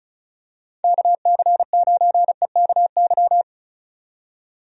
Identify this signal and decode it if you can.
Morse code, heard as a keyed tone, about 36 wpm, 700 Hz. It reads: KC9EKY
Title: USA Call Signs - 6 Characters - 35wpm